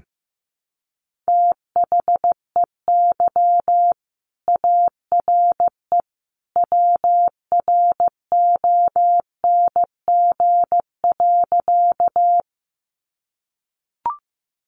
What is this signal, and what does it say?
Morse code, heard as a keyed tone, about 15 wpm, 700 Hz. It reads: THEY ARE WRONG.
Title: Sentences from Top 900 Words 15wpm